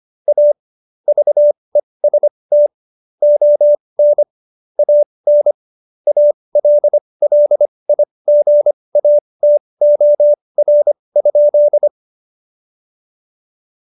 Morse code, heard as a keyed tone, about 25 wpm, spaced as for 20 wpm, 590 Hz. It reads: A VEST ON AN ALLIGATOR?
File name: Jokes-25-char-20-eff